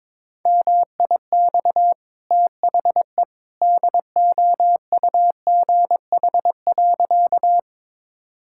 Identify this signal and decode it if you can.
Morse code, heard as a keyed tone, about 21 wpm, 690 Hz. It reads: MIX THE DOUGH.
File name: practice / 22wpm / Easy Sentences with Hints - 22wpm